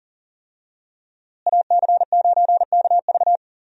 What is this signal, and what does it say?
Morse code, heard as a keyed tone, about 40 wpm, 700 Hz. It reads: AC9KV
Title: USA Call Signs - 5 Characters - Repeated 2x - 40wpm